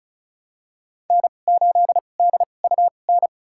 Morse code, heard as a keyed tone, about 34 wpm, spaced as for 25 wpm, 700 Hz. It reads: N8DUN
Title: USA Call Signs - 5 Characters - 2x Character Spacing - 35wpm